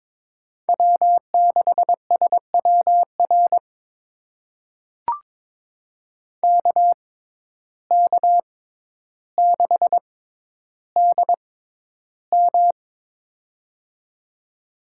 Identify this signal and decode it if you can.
Morse code, heard as a keyed tone, about 22 wpm, spaced as for 11 wpm, 700 Hz. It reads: W6SWR KK6DM
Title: USA Call Signs - 5 Characters - 6x Character Spacing - 22wpm